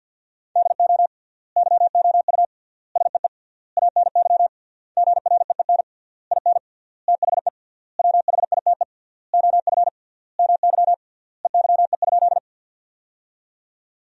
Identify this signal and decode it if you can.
Morse code, heard as a keyed tone, about 53 wpm, 700 Hz. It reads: DO YOU SEE ANY GREEN IN THE WHITE OF MY EYE?
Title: 3x Word-Spacing - Sentences from Top 500 Words - 50wpm